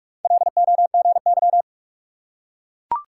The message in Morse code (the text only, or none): LOGY